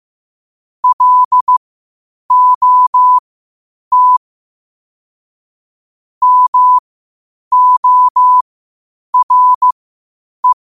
Morse code, heard as a keyed tone, about 15 wpm, spaced as for 8 wpm, 1000 Hz. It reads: LOT MORE
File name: qotdfiles / 8wpm